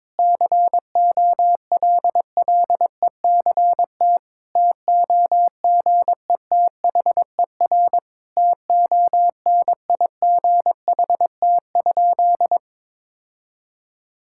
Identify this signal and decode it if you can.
Morse code, heard as a keyed tone, about 22 wpm, 700 Hz. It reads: COLLECT TOGETHER TONIGHT?
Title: Sentences from Top 1600 Words 22wpm